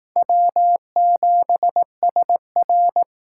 Morse code, heard as a keyed tone, about 18 wpm, 700 Hz. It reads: W7SR
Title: Speed Racing - USA 1x2 Call Signs 15wpm